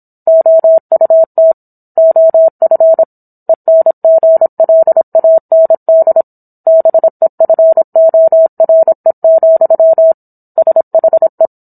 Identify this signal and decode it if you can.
Morse code, heard as a keyed tone, about 27 wpm, 650 Hz. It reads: OUT OF ENGLAND BEFORE, SHE